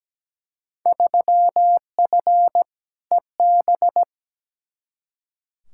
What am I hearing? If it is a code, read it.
Morse code, heard as a keyed tone, about 17 wpm, 700 Hz. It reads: 3F EB
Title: Field Day Exchanges - 17wpm